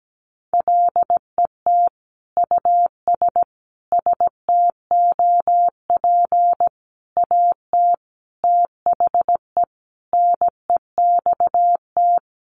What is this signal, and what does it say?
Morse code, heard as a keyed tone, about 17 wpm, 700 Hz. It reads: LET US STOP AT THE NEXT
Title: Sentences from Top 400 Words 17wpm